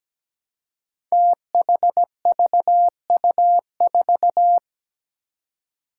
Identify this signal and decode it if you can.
Morse code, heard as a keyed tone, about 17 wpm, 700 Hz. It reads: THVU4
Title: Five Challenging Letter-Number Combinations 17wpm